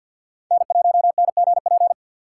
Morse code, heard as a keyed tone, about 53 wpm, 700 Hz. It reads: N1NGP
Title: USA Call Signs - 5 Characters - 2x Character Spacing - 50wpm